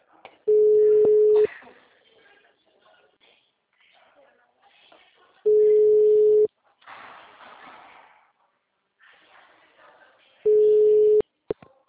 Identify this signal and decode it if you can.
Ringback tone (Europe, 425 Hz), with speech.
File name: out- -2010- - - .1810